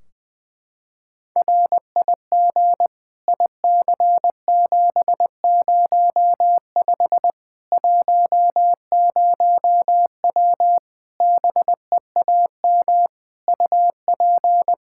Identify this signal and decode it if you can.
Morse code, heard as a keyed tone, about 21 wpm, 700 Hz. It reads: RIG IC705 10W BEAM UP
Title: Practice QSO - Exchange 2 - Info - 20wpm